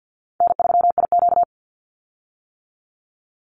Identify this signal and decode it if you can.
Morse code, heard as a keyed tone, about 50 wpm, 700 Hz. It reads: N3IQ
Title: Call Signs - 50wpm